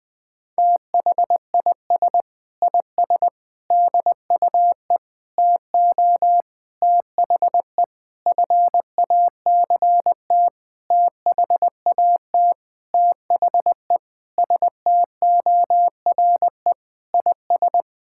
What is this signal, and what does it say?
Morse code, heard as a keyed tone, about 20 wpm, 700 Hz. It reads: THIS IS DUE TO THE FACT THAT THE STORE IS
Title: Sentences with 7-Word Phrases - 20wpm